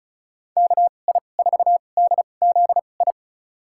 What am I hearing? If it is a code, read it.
Morse code, heard as a keyed tone, about 36 wpm, spaced as for 25 wpm, 700 Hz. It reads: KI4DZI
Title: USA Call Signs - 6 Characters - 2x Character Spacing - 35wpm